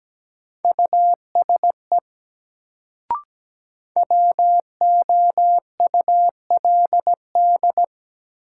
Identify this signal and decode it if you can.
Morse code, heard as a keyed tone, about 17 wpm, 700 Hz. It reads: USE WOULD
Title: Sentences from Top 100 Words - New Word Drill - 17wpm